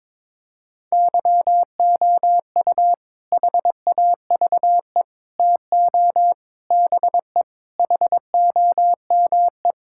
Morse code, heard as a keyed tone, about 22 wpm, 700 Hz. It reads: YOU HAVE TO BE HOME
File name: practice / 22wpm / Sentences with 4-Word Phrases - 22wpm